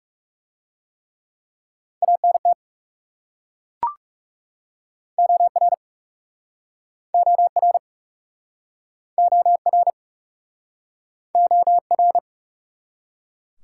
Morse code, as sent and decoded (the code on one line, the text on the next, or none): .- -. - / --- .-. / --- .-. / --- .-. / --- .-.
ANT OR OR OR OR